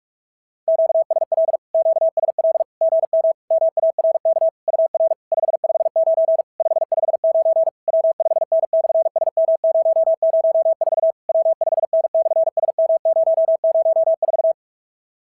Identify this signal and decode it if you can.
Morse code, heard as a keyed tone, about 48 wpm, 650 Hz. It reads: QSL QSL GM MARK UR 559 559 W5N/SM004 W5N/SM004
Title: SOTA QSO Practice 45wpm